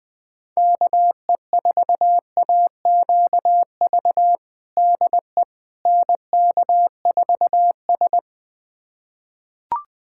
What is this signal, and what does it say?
Morse code, heard as a keyed tone, about 20 wpm, 700 Hz. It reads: KE4AQV DE NK4S
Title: USA Callsign DE Callsign 20wpm